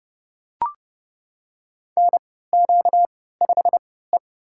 Morse code, heard as a keyed tone, about 32 wpm, spaced as for 17 wpm, 700 Hz. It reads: NQ5E
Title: USA Call Signs - 4 Characters - 3x Character Spacing - 30wpm